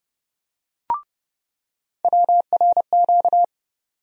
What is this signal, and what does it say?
Morse code, heard as a keyed tone, about 30 wpm, 700 Hz. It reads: WRQ
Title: USA Call Sign Postfixes - 3 Letters 30wpm